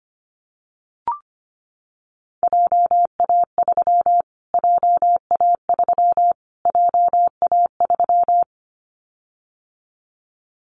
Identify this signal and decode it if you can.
Morse code, heard as a keyed tone, about 25 wpm, 700 Hz. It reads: JA3 JA3 JA3
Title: Callsign Prefixes for Top 25 DXCC - Repeated 3x - 25wpm